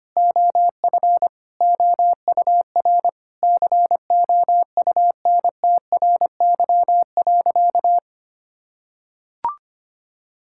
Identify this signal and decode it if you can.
Morse code, heard as a keyed tone, about 25 wpm, 700 Hz. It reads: OF OUR COUNTRY.
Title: Sentences from Top 1000 Words - Review Part 6 - 25wpm